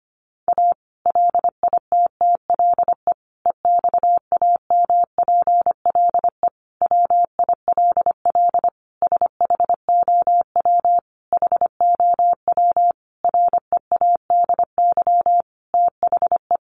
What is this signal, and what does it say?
Morse code, heard as a keyed tone, about 25 wpm, 700 Hz. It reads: A LITTLE EXAMPLE WILL SHOW HOW READY THE
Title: Sentences from Top 1500 Words 25wpm